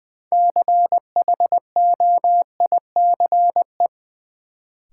Morse code, heard as a keyed tone, about 21 wpm, 700 Hz. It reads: CHOICE
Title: Sentences from Top 900 Words - New Word Drill - 20wpm